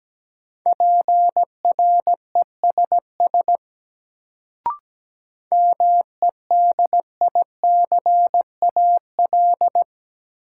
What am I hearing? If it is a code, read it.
Morse code, heard as a keyed tone, about 17 wpm, 700 Hz. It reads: PRESS MEDICAL
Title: Sentences from Top 800 Words - New Word Drill - 17wpm